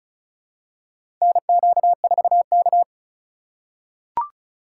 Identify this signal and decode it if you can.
Morse code, heard as a keyed tone, about 36 wpm, 700 Hz. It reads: NQ4K
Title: USA Call Signs - 4 Characters - 2x Character Spacing - 35wpm